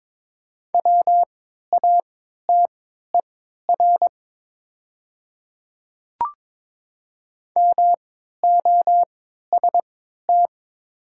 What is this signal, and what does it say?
Morse code, heard as a keyed tone, about 21 wpm, spaced as for 12 wpm, 700 Hz. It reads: WATER MOST
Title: 3x Character Spacing - Top 100 Words - 22wpm